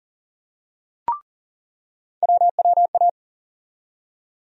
Morse code, heard as a keyed tone, about 39 wpm, 700 Hz. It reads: WWA
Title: USA Call Sign Postfixes - 3 Letters 40wpm